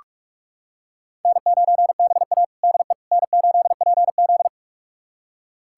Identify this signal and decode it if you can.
Morse code, heard as a keyed tone, about 46 wpm, 700 Hz. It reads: N9BA DE N8PZ
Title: USA Callsign DE Callsign - 4 Characters - 45wpm